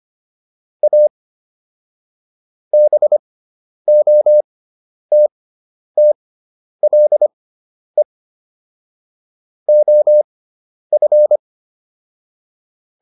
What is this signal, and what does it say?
Morse code, heard as a keyed tone, about 25 wpm, spaced as for 10 wpm, 600 Hz. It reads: A BOTTLE OF